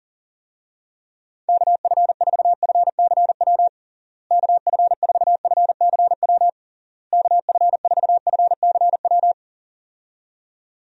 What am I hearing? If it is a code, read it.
Morse code, heard as a keyed tone, about 40 wpm, 700 Hz. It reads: KF4FCW KF4FCW KF4FCW
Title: USA Call Signs - 6 Characters - Repeated 3x - 40wpm